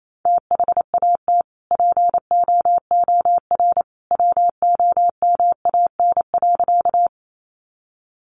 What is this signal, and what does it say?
Morse code, heard as a keyed tone, about 28 wpm, 700 Hz. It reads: THAT POOR WOMAN.
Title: Sentences from Top 1100 Words 28wpm